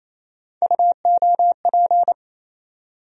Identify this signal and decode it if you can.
Morse code, heard as a keyed tone, about 27 wpm, 700 Hz. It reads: UOP
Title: USA Call Sign Postfixes - 3 Letters 28wpm